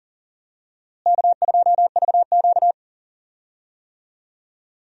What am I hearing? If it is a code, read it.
Morse code, heard as a keyed tone, about 40 wpm, 700 Hz. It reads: K2VQ
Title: USA Call Signs - 4 Characters - Repeated 2x - 40wpm